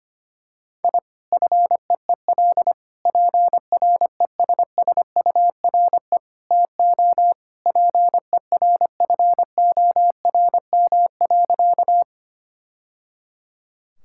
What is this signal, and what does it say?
Morse code, heard as a keyed tone, about 25 wpm, 700 Hz. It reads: I FEEL PRESSURE TO PERFORM.
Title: Sentences from Top 1000 Words - Review Part 6 - 25wpm